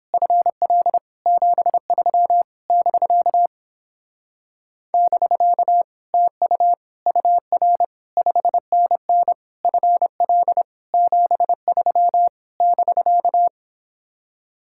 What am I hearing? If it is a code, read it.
Morse code, heard as a keyed tone, about 31 wpm, 700 Hz. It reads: FL 73 <BK> <BK> TU UR 5NN FL 73 <BK>